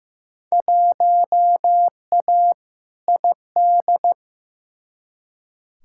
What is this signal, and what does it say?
Morse code, heard as a keyed tone, about 15 wpm, 700 Hz. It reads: 1A ID
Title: Field Day Exchanges Encore - 15wpm